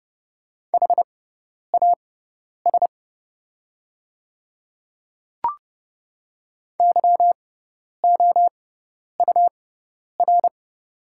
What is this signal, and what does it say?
Morse code, heard as a keyed tone, about 30 wpm, spaced as for 10 wpm, 700 Hz. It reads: HAS YOUR
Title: 6x Character Spacing - Top 100 Words - 30wpm